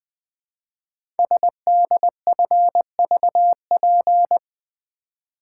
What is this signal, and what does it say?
Morse code, heard as a keyed tone, about 20 wpm, 700 Hz. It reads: SDFVP
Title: Code Groups - 5 Letter:Numbers - 20wpm